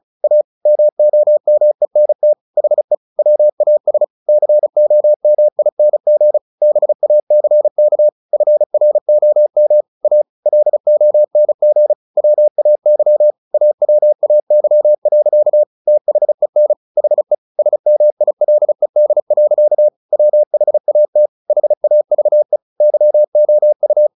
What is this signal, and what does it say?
Morse code, heard as a keyed tone, about 34 wpm, 600 Hz. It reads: A MOMENT HE WAS COMING BACK FROM A LONG WAY AWAY. THEN HE SMILED. WHAT HAVE YOU
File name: the old man and the sea / the old man and the sea 1 35wpm